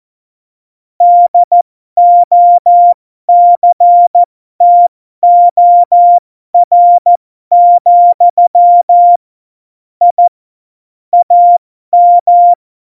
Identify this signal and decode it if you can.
Morse code, heard as a keyed tone, about 14 wpm, spaced as for 12 wpm, 700 Hz. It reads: DOCTOR, I AM
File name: Morsekurs / Witze